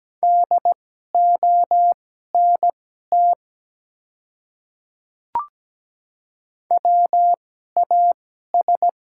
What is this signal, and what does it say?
Morse code, heard as a keyed tone, about 17 wpm, spaced as for 12 wpm, 690 Hz. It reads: DONT WAS
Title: 2x Character Spacing - Top 300 Words - 17wpm